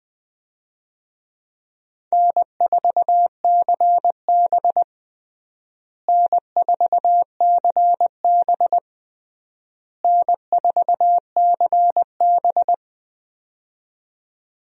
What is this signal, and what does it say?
Morse code, heard as a keyed tone, about 21 wpm, 700 Hz. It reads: N4CB N4CB N4CB
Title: USA Call Signs - 4 Characters - Repeated 3x - 20wpm